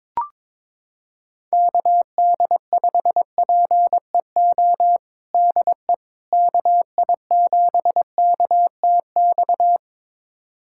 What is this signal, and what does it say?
Morse code, heard as a keyed tone, about 21 wpm, 700 Hz. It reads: KD5PEO DE KI7KTX
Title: USA Callsign DE Callsign - 6 Characters - 22wpm